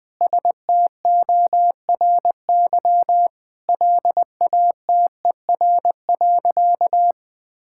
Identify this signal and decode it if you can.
Morse code, heard as a keyed tone, about 20 wpm, 700 Hz. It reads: STORY LATER.
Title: Sentences with 4-Word Phrases - 20wpm